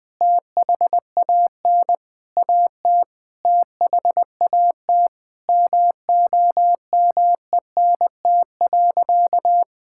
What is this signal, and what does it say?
Morse code, heard as a keyed tone, about 20 wpm, 700 Hz. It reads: THAN AT THAT MOMENT.
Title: Sentences from Top 1000 Words - Review Part 6 - 20wpm